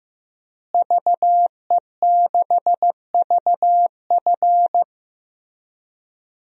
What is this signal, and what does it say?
Morse code, heard as a keyed tone, about 14 wpm, 700 Hz. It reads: VE6VF
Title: DXCC-100 Call Signs - 5 Characters - Repeated 2x - 15wpm